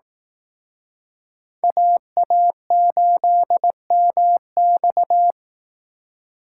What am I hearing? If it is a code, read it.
Morse code, heard as a keyed tone, about 18 wpm, 700 Hz. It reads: AA8MX
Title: Speed Racing - USA 2x2 Call Signs 15wpm